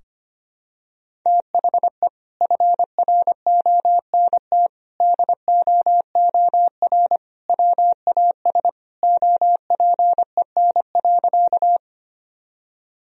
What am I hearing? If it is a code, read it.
Morse code, heard as a keyed tone, about 25 wpm, 700 Hz. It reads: THE FRONT DOOR WAS OPEN.
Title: Sentences from Top 1000 Words - Review Part 2 - 25wpm